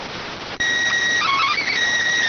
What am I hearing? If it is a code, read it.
SSTV (slow-scan television), PD 290.